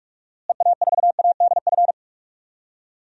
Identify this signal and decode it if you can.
Morse code, heard as a keyed tone, about 47 wpm, 700 Hz. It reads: EA4ADF
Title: DXCC-100 Call Signs - 6 Characters - Repeated 2x - 45wpm